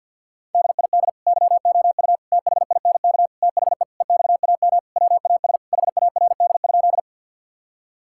Morse code, heard as a keyed tone, about 51 wpm, 690 Hz. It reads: DID YOU THINK THE EXAM WAS HARD?
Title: Sentences from Top 1000 Words - Review Part 5 - 50wpm